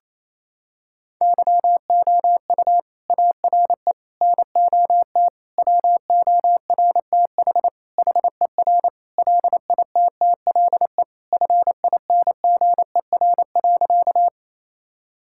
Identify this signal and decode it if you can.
Morse code, heard as a keyed tone, about 28 wpm, 700 Hz. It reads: YOU ARE NOT WORTH HER LITTLE FINGER.
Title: Sentences from Top 1600 Words 28wpm